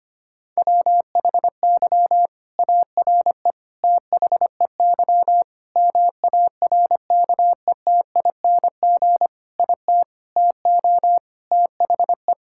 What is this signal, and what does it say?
Morse code, heard as a keyed tone, about 25 wpm, 700 Hz. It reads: WHY ARE THEY MARKETING IT TO THE
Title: Sentences from Top 1000 Words - Review Part 5 - 25wpm